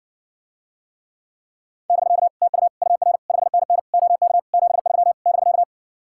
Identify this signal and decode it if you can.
Morse code, heard as a keyed tone, about 70 wpm, 700 Hz. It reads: <BK> TU UR 5NN OK 73 <BK>